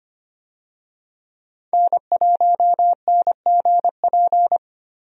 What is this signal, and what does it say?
Morse code, heard as a keyed tone, about 25 wpm, 700 Hz. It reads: N1NGP
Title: USA Call Signs - 5 Characters - 25wpm